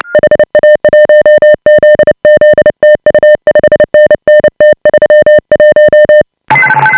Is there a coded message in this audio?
Yes: Morse code from 0.1 to 6.2 s, heard as a keyed tone, about 29 wpm, 590 Hz. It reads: HA1ZZ TU 5NN T31